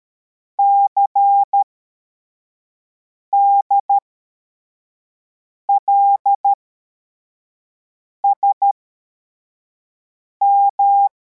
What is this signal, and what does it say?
Morse code, heard as a keyed tone, about 13 wpm, spaced as for 4 wpm, 800 Hz. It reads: CDLSM
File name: formation / cw / Text85